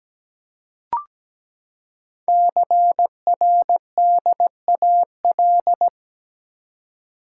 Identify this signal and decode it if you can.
Morse code, heard as a keyed tone, about 17 wpm, 700 Hz. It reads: CRDAL
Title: Code Groups - 5 Letters - 17wpm